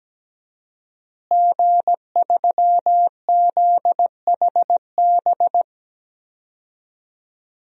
Morse code, heard as a keyed tone, about 17 wpm, 700 Hz. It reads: G3ZHB
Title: DXCC-100 Call Signs - 5 Characters - Repeated 3x - 17wpm